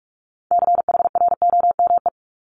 Morse code, heard as a keyed tone, about 46 wpm, 700 Hz. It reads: CHROME
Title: Multi-letter Phonemes Repeated 3x 45wpm